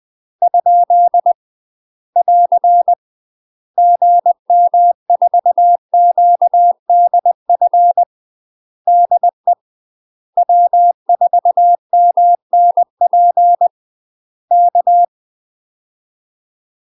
Morse code, heard as a keyed tone, about 20 wpm, 700 Hz. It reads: ? <AR> GM4QDF DE W4MNP K
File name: UK MASTER QSOs TRAINING / 20WPM / UK MASTER QSOs 0021